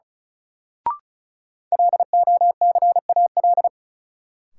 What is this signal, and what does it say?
Morse code, heard as a keyed tone, about 36 wpm, 700 Hz. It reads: LOCAL
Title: Sentences from Top 1500 Words - New Word Drill Alt - 35wpm